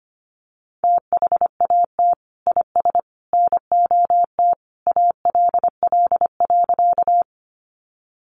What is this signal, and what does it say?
Morse code, heard as a keyed tone, about 25 wpm, 700 Hz. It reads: THAT IS NOT ALL.
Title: Sentences from Top 600 Words 25wpm